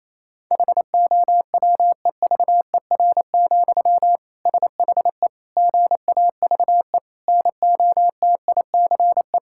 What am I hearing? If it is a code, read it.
Morse code, heard as a keyed tone, about 28 wpm, 700 Hz. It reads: HOWEVER, SHE GAVE NOTICE
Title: Sentences from Top 1000 Words - Review Part 6 - 28wpm